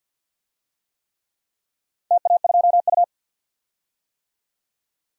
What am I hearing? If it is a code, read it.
Morse code, heard as a keyed tone, about 51 wpm, 700 Hz. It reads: TA2U
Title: DXCC-100 Call Signs - 4 Characters - 50wpm